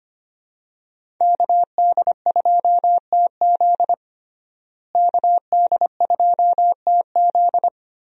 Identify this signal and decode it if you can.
Morse code, heard as a keyed tone, about 25 wpm, 700 Hz. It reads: KD2TZ KD2TZ
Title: USA Call Signs - 5 Characters - Repeated 3x - 25wpm